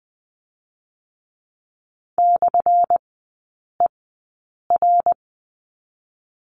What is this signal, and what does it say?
Morse code, heard as a keyed tone, about 19 wpm, 700 Hz. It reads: / E R
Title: 218 Lesson 35e - Number 0 - Three Characters 20wpm